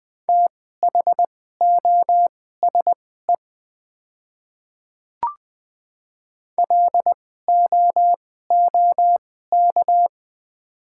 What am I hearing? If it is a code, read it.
Morse code, heard as a keyed tone, about 20 wpm, spaced as for 14 wpm, 700 Hz. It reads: THOSE LOOK
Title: 2x Character Spacing - Top 300 Words - 20wpm